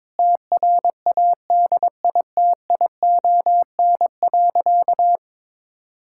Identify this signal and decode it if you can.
Morse code, heard as a keyed tone, about 22 wpm, 700 Hz. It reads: TRADITION.
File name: Sentences with 7-Word Phrases - 22wpm